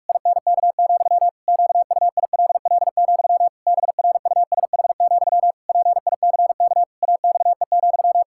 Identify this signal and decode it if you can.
Morse code, heard as a keyed tone, about 46 wpm, 700 Hz. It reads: INK, QUILL, BRUSH, PICK AXE,
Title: Pangram Challenge v2 - 45wpm